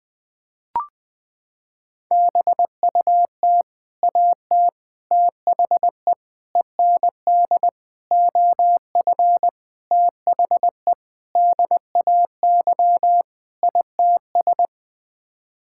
Morse code, heard as a keyed tone, about 20 wpm, 700 Hz. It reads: BUT AT THE END OF THE DAY ITS
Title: Sets of 8 Words - 20wpm